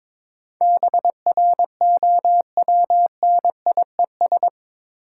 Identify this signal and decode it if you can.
Morse code, heard as a keyed tone, about 22 wpm, 690 Hz. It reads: BROWNIES
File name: Difficult Words with Hints - 22wpm